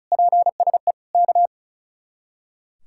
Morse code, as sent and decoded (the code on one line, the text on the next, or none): .--. ... / -.-
PSE K